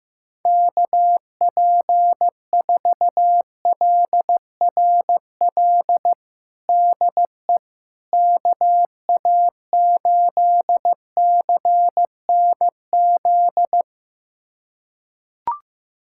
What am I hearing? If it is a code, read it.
Morse code, heard as a keyed tone, about 15 wpm, 700 Hz. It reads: KP4LRL DE KA8CNZ
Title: USA Callsign DE Callsign 15wpm